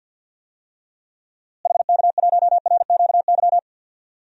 Morse code, heard as a keyed tone, about 49 wpm, 700 Hz. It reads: HK1RQY
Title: DXCC-100 Call Signs - 6 Characters - Repeated 2x - 50wpm